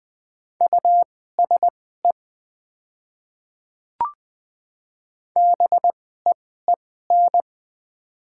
Morse code, heard as a keyed tone, about 20 wpm, spaced as for 15 wpm, 700 Hz. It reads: USE BEEN
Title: 2x Character Spacing - Top 100 Words - 20wpm